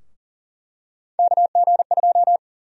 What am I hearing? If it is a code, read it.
Morse code, heard as a keyed tone, about 40 wpm, 700 Hz. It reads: KG2
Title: USA Call Sign Prefixes - Repeated 2x - 40wpm